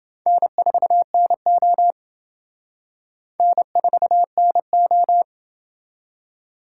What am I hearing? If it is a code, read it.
Morse code, heard as a keyed tone, about 28 wpm, 700 Hz. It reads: N4NO N4NO